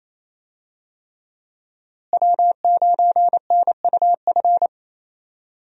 Morse code, heard as a keyed tone, about 28 wpm, 700 Hz. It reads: W9NUF
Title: USA Call Signs - 5 Characters - Repeated 3x - 28wpm